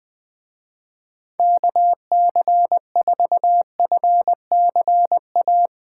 Morse code, heard as a keyed tone, about 20 wpm, 700 Hz. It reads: KC4FCA
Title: Speed Racing - USA 2x3 Call Signs 20wpm